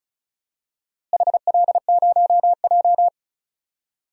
Morse code, heard as a keyed tone, about 36 wpm, 700 Hz. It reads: HL0J
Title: DXCC-100 Call Signs - 4 Characters - Repeated 3x - 35wpm